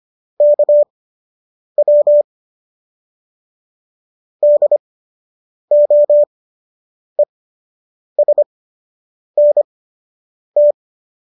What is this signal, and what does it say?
Morse code, heard as a keyed tone, about 25 wpm, spaced as for 8 wpm, 590 Hz. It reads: KW DOESNT